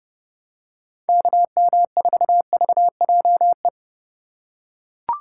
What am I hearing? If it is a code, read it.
Morse code, heard as a keyed tone, about 30 wpm, 700 Hz. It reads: KM4VJE
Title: USA Call Signs - 6 Characters - 2x Character Spacing - 30wpm